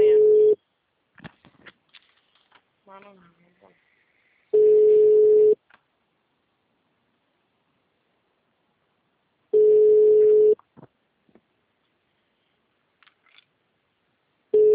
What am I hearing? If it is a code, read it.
Ringback tone (Europe, 425 Hz), with speech.
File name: out- -2003- - - .1562